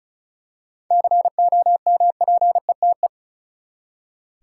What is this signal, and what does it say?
Morse code, heard as a keyed tone, about 34 wpm, 700 Hz. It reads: COMPETE
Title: Seven Letter Words - 35wpm